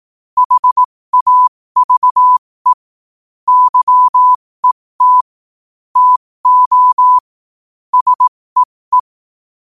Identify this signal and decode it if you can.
Morse code, heard as a keyed tone, about 18 wpm, spaced as for 15 wpm, 1000 Hz. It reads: HAVE YET TO SEE
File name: qotdfiles / 15wpm